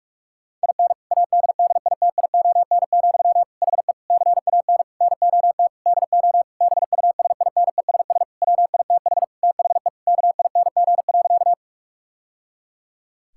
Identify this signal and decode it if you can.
Morse code, heard as a keyed tone, about 46 wpm, 700 Hz. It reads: IN ADDITION, HE CAN NOT DO BUSINESS WITH THE KING.
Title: Sentences from Top 1000 Words - Review Part 6 - 45wpm